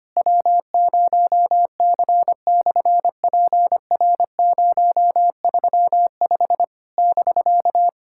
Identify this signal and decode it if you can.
Morse code, heard as a keyed tone, about 24 wpm, 700 Hz. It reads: W0C/PR035 <BK>
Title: SOTA QSO Practice 25wpm